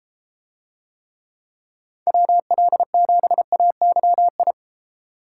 Morse code, heard as a keyed tone, about 33 wpm, 700 Hz. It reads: WL7AYI